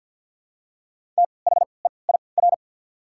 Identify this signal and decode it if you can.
Morse code, heard as a keyed tone, about 48 wpm, spaced as for 28 wpm, 700 Hz. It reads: THEIR
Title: 3x Character Spacing - Top 200 Words - 50wpm